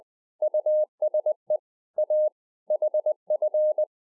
Morse code, heard as a keyed tone, about 20 wpm, 610 Hz. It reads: USE A HF